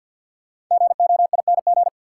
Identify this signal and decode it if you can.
Morse code, heard as a keyed tone, about 53 wpm, 700 Hz. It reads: GOING
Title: Sets of 2 Words - New Word Drill - 50wpm